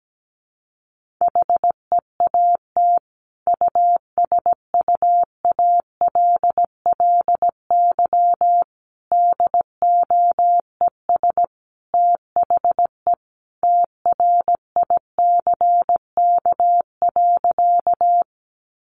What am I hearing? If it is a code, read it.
Morse code, heard as a keyed tone, about 17 wpm, 700 Hz. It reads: HEAT USUALLY DOES THE TRICK.
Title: Sentences from Top 1500 Words 17wpm